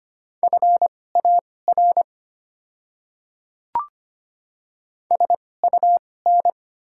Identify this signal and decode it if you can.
Morse code, heard as a keyed tone, about 25 wpm, spaced as for 18 wpm, 690 Hz. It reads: FAR SUN
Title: 2x Character Spacing - Top 300 Words - 25wpm